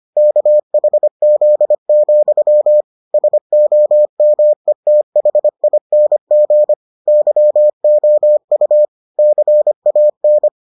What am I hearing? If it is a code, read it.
Morse code, heard as a keyed tone, about 25 wpm, 600 Hz. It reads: KHZ, SOMETHING YOU CAN